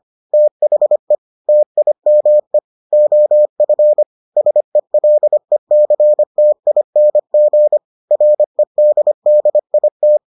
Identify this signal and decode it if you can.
Morse code, heard as a keyed tone, about 25 wpm, 600 Hz. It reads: THE TIME OF SELECTING REDDIT